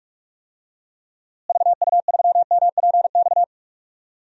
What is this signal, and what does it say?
Morse code, heard as a keyed tone, about 46 wpm, 700 Hz. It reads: VU3MPX
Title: DXCC-100 Call Signs - 6 Characters - Repeated 2x - 45wpm